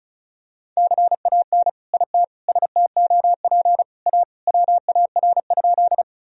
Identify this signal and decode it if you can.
Morse code, heard as a keyed tone, about 36 wpm, 700 Hz. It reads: CAN IT STOP A WAR?
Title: Sentences from Top 1000 Words - Review Part 2 - 35wpm